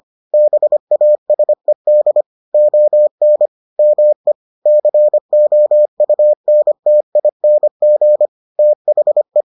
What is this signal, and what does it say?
Morse code, heard as a keyed tone, about 25 wpm, 600 Hz. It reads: BASED ON ME COUNTING THE